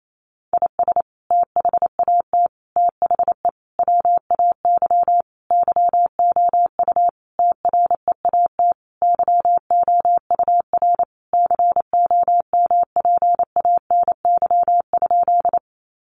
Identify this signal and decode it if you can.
Morse code, heard as a keyed tone, about 28 wpm, 700 Hz. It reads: IS THAT THE WAY YOU TREAT YOUR COMPANY?
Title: Sentences from Top 1300 Words 28wpm